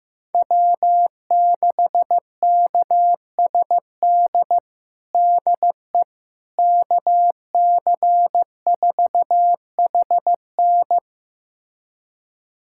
Morse code, heard as a keyed tone, about 15 wpm, 700 Hz. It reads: W6KSD DE KC4HN
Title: USA Callsign DE Callsign - 5 Characters - 15wpm